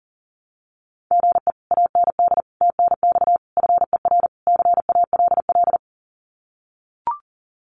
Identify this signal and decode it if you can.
Morse code, heard as a keyed tone, about 40 wpm, 700 Hz. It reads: GE AND TNX FER CALL
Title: Common QSO Phrases Encore 40wpm